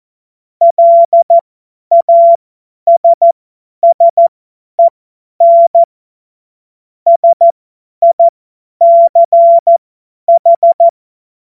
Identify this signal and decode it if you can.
Morse code, heard as a keyed tone, about 14 wpm, spaced as for 10 wpm, 680 Hz. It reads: LASSEN SICH
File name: Morsekurs / Witze / Witz 02. August 2022 50 F70